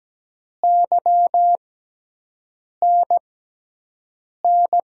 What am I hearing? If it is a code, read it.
Morse code, heard as a keyed tone, about 17 wpm, spaced as for 6 wpm, 700 Hz. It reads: YNN
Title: USA Call Signs - 6 Characters - 6x Character Spacing - 17wpm